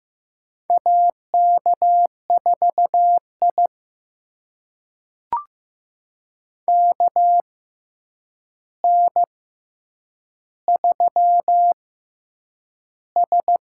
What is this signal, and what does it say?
Morse code, heard as a keyed tone, about 15 wpm, 700 Hz. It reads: A K 4 I K N 3 S
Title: USA Call Signs - 4 Characters - 6x Character Spacing - 15wpm